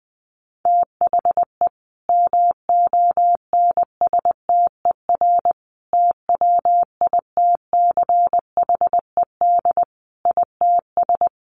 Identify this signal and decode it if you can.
Morse code, heard as a keyed tone, about 20 wpm, 700 Hz. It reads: THE MONSTER TWITCHED ITS
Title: A Sound Of Thunder 20wpm